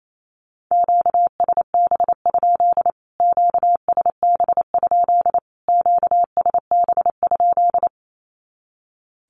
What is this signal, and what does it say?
Morse code, heard as a keyed tone, about 28 wpm, 700 Hz. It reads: QSB? QSB? QSB?
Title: Q-Codes Repeated 3x 28wpm